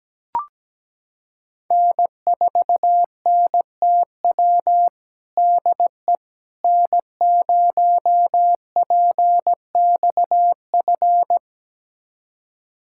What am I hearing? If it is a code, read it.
Morse code, heard as a keyed tone, about 17 wpm, 700 Hz. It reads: N4NTW DE N0PXF
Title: USA Callsign DE Callsign - 5 Characters - 17wpm